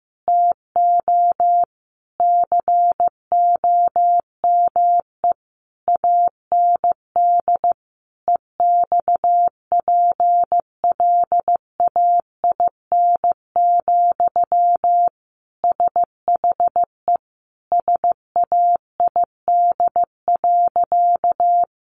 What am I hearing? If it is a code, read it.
Morse code, heard as a keyed tone, about 15 wpm, 700 Hz. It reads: TO COME AND EXPLAIN, SHE SAID.
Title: Sentences from Top 1100 Words 15wpm